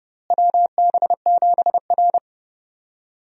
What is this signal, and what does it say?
Morse code, heard as a keyed tone, about 31 wpm, 700 Hz. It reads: WB7R
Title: USA Call Signs - 4 Characters - Repeated 2x - 30wpm